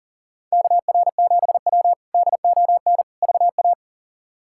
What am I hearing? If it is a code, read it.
Morse code, heard as a keyed tone, about 40 wpm, 700 Hz. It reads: KR7W DON VA
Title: K1USN SST QSO Practice - 40wpm